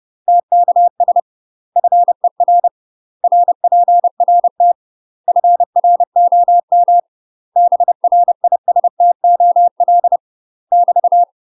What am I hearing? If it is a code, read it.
Morse code, heard as a keyed tone, about 30 wpm, 700 Hz. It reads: TKS FER RPRT FROM BRISTOL <BT>
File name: UK MASTER QSOs TRAINING / 30WPM / UK MASTER QSOs 0041